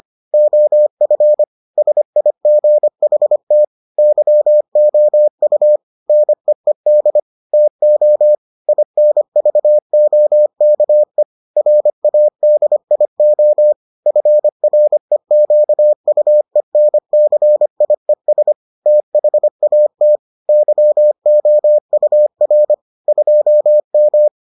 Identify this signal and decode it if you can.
Morse code, heard as a keyed tone, about 25 wpm, 600 Hz. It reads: OF SIGHT YOU NEED TO INVOKE RADIO FREQUENCIES THAT YOUR 2M